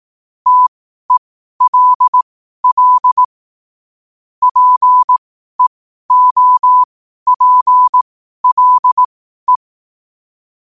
Morse code, heard as a keyed tone, about 18 wpm, spaced as for 12 wpm, 1000 Hz. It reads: TELL PEOPLE
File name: qotdfiles / 12wpm